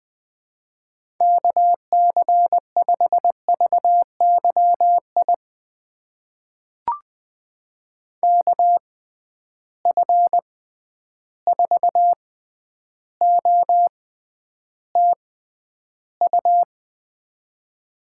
Morse code, heard as a keyed tone, about 20 wpm, spaced as for 10 wpm, 700 Hz. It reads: KC5VYI KF4OTU
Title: USA Call Signs - 6 Characters - 6x Character Spacing - 20wpm